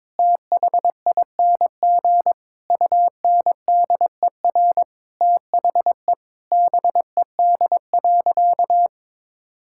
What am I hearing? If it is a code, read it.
Morse code, heard as a keyed tone, about 22 wpm, 700 Hz. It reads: THING UNDER THE BED.
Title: Sentences from Top 1000 Words - Review Part 5 - 22wpm